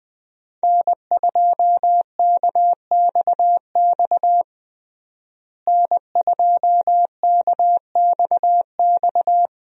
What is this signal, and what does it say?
Morse code, heard as a keyed tone, about 21 wpm, 700 Hz. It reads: N2KXX N2KXX
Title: USA Call Signs - 5 Characters - Repeated 3x - 20wpm